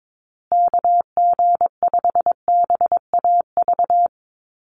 Morse code, heard as a keyed tone, about 22 wpm, 700 Hz. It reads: KG5BAV
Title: USA Call Signs Repeated 2x 22wpm